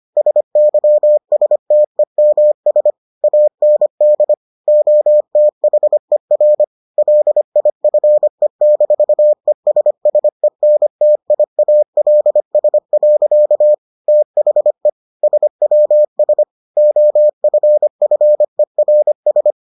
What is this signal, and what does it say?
Morse code, heard as a keyed tone, about 25 wpm, 600 Hz. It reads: SYSTEMS AND OTHER LIFE-ESSENTIALS. THE SWS OFFERS